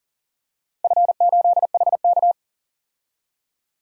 Morse code, heard as a keyed tone, about 41 wpm, 700 Hz. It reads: F8HK
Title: DXCC-100 Call Signs - 4 Characters - 40wpm